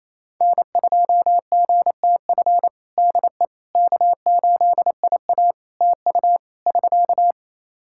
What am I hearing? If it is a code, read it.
Morse code, heard as a keyed tone, about 28 wpm, 700 Hz. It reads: N2GTF DE K8IA TU <SK>
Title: Practice QSO - Exchange 4 - Ending - 28wpm